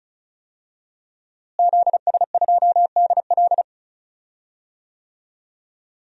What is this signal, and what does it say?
Morse code, heard as a keyed tone, about 36 wpm, 700 Hz. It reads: ZS2DL
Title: DXCC-100 Call Signs - 5 Characters - Repeated 2x - 35wpm